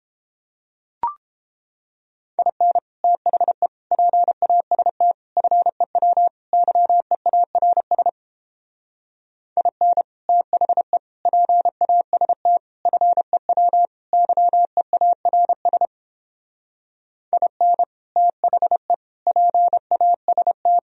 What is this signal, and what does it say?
Morse code, heard as a keyed tone, about 31 wpm, 700 Hz. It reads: IN THE PAST FEW YEARS IN THE PAST FEW YEARS IN THE PAST